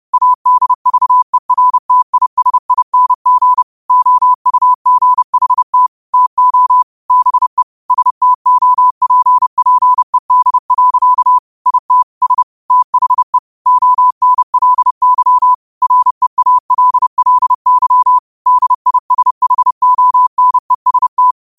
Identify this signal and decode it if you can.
Morse code, heard as a keyed tone, about 30 wpm, 1000 Hz. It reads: ADVERTISING OUGHT TO BE STOPPED. IT S THE ONLY REALLY DISHONEST
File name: qotdfiles / 30wpm